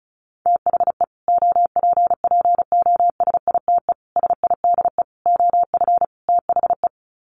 Morse code, heard as a keyed tone, about 34 wpm, 700 Hz. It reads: THE OPPOSITE SIDE OF THE
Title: Sets of 5 Words Encore 35wpm